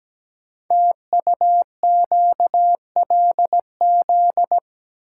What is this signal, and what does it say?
Morse code, heard as a keyed tone, about 17 wpm, 700 Hz. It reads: TUQLZ
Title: Code Groups - 5 Letters - 17wpm